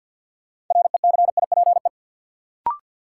Morse code, heard as a keyed tone, about 50 wpm, 700 Hz. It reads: RECIPE
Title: Sentences from Top 1200 Words - New Word Drill - 50wpm